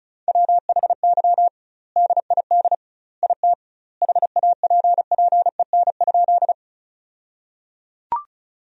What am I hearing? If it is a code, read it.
Morse code, heard as a keyed tone, about 36 wpm, 700 Hz. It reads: WHY DID IT HAPPEN?
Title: 2x Word-Spacing - Sentences from Top 400 Words - 35wpm